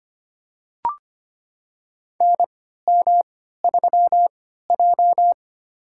Morse code, heard as a keyed tone, about 25 wpm, spaced as for 14 wpm, 700 Hz. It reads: NM3J
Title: USA Call Signs - 4 Characters - 3x Character Spacing - 25wpm